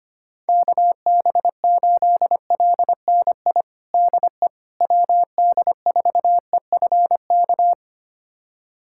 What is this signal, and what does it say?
Morse code, heard as a keyed tone, about 25 wpm, 700 Hz. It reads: KB8LNI DE WD4EFK
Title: USA Callsign DE Callsign 25wpm